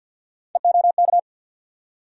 Morse code, heard as a keyed tone, about 46 wpm, 700 Hz. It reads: EOK
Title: USA Call Sign Postfixes - 3 Letters 50wpm